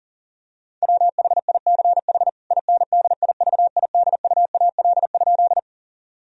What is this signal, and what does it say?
Morse code, heard as a keyed tone, about 40 wpm, 690 Hz. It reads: WHICH INDIVIDUAL?
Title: Sentences from Top 1000 Words - Review Part 4 - 40wpm